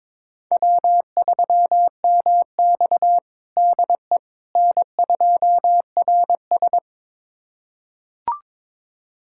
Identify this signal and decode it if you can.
Morse code, heard as a keyed tone, about 22 wpm, 700 Hz. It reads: W3MX DE N2RS
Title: USA Callsign DE Callsign - 4 Characters - 22wpm